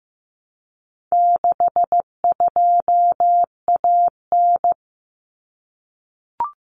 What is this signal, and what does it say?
Morse code, heard as a keyed tone, about 15 wpm, 700 Hz. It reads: 62AN
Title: Common Cut Numbers - 4-Digit 15wpm